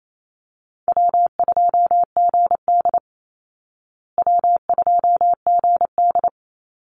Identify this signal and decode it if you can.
Morse code, heard as a keyed tone, about 28 wpm, 700 Hz. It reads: W2GD W2GD
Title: Easy Call Signs Repeated 2x - 28wpm